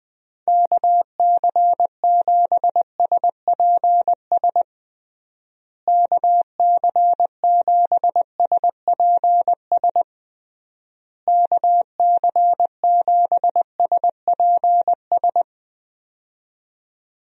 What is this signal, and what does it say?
Morse code, heard as a keyed tone, about 20 wpm, 700 Hz. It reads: KC7SPS KC7SPS KC7SPS
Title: USA Call Signs - 6 Characters - Repeated 3x - 20wpm